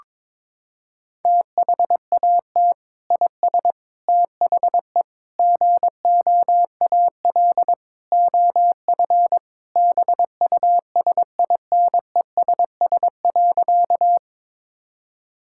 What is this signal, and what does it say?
Morse code, heard as a keyed tone, about 22 wpm, 700 Hz. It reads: THAT IS THE GOAL OF BUSINESS.
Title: Sentences from Top 1000 Words - Review Part 5 - 22wpm